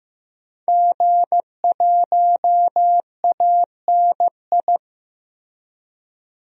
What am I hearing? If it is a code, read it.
Morse code, heard as a keyed tone, about 15 wpm, 700 Hz. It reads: G1ANI
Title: DXCC-100 Call Signs - 5 Characters - Repeated 2x - 15wpm